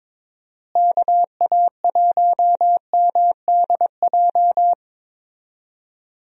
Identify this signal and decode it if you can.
Morse code, heard as a keyed tone, about 21 wpm, 700 Hz. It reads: KA1MDJ
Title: USA Call Signs - 6 Characters - Repeated 3x - 22wpm